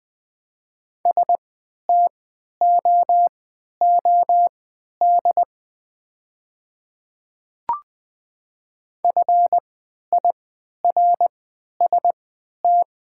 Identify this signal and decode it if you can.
Morse code, heard as a keyed tone, about 20 wpm, spaced as for 11 wpm, 700 Hz. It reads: STOOD FIRST
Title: 3x Character Spacing - Top 500 Words - 20wpm